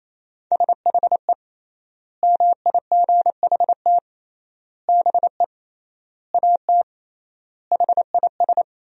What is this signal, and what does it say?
Morse code, heard as a keyed tone, about 27 wpm, 700 Hz. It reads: SHE MIGHT BE AT HIS
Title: 3x Word-Spacing - Sentences from Top 300 Words - 28wpm